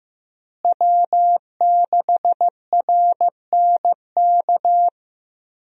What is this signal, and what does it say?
Morse code, heard as a keyed tone, about 15 wpm, 700 Hz. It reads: W6RNK
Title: USA Call Signs - 5 Characters - Repeated 2x - 15wpm